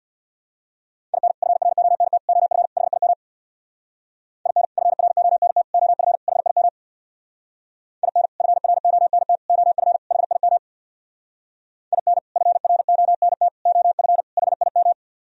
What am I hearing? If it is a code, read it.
Morse code, heard as a keyed tone, about 64 wpm, 700 Hz. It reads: IN FRONT OF HIM IN FRONT OF HIM IN FRONT OF HIM IN FRONT OF HIM